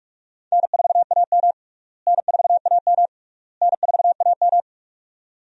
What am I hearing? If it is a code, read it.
Morse code, heard as a keyed tone, about 48 wpm, 700 Hz. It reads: N4AM N4AM N4AM
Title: USA Call Signs - 4 Characters - Repeated 3x - 45wpm